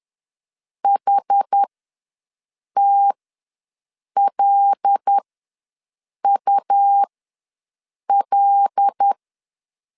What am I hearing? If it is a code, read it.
Morse code, heard as a keyed tone, about 10 wpm, spaced as for 6 wpm, 800 Hz. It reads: HTLUL
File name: LEZIONI / 5 LBVC / 5 L ws 10wpm 3636